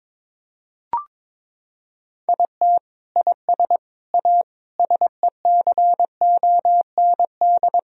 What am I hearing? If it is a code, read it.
Morse code, heard as a keyed tone, about 22 wpm, 700 Hz. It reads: IT IS A SECOND
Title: Sentences from Top 1000 Words - Review Part 2 - 22wpm